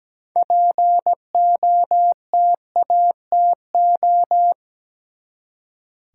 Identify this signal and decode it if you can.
Morse code, heard as a keyed tone, about 17 wpm, 700 Hz. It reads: POTATO
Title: Six Letter Words - 17wpm